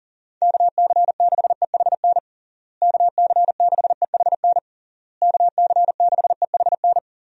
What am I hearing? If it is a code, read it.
Morse code, heard as a keyed tone, about 37 wpm, 700 Hz. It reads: KC6EHN KC6EHN KC6EHN
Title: USA Call Signs - 6 Characters - Repeated 3x - 40wpm